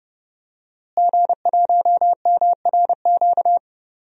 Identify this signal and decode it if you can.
Morse code, heard as a keyed tone, about 30 wpm, 700 Hz. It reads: G1MRQ
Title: DXCC-100 Call Signs - 5 Characters - Repeated 2x - 30wpm